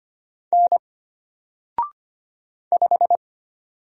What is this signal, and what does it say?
Morse code, heard as a keyed tone, about 25 wpm, 700 Hz. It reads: N 5
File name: Single-Letter-Number-25wpm